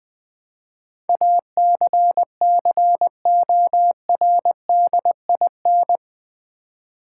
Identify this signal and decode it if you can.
Morse code, heard as a keyed tone, about 20 wpm, 700 Hz. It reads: ACCORDIN
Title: Build A Word - 4 Letters At A Time - 11-Letter Words - 20wpm